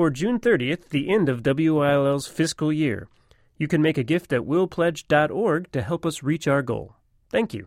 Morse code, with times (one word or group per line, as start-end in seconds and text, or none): none